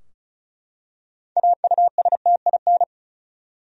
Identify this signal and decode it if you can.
Morse code, heard as a keyed tone, about 34 wpm, 700 Hz. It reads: AUSTIN
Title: USA Common Names - 35wpm